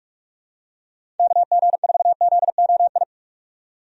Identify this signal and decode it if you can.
Morse code, heard as a keyed tone, about 47 wpm, 700 Hz. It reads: KG4ZOI
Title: USA Call Signs - 6 Characters - 45wpm